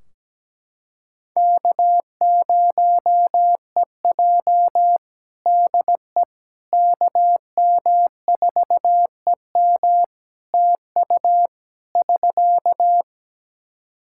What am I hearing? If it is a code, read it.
Morse code, heard as a keyed tone, about 17 wpm, 700 Hz. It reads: K0EJ DE KM4EM TU <SK>
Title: Practice QSO - Exchange 4 - Ending - 17wpm